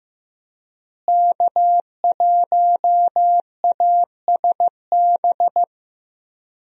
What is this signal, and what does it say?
Morse code, heard as a keyed tone, about 16 wpm, 700 Hz. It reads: K1ASB
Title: USA Call Signs - 5 Characters - Repeated 3x - 15wpm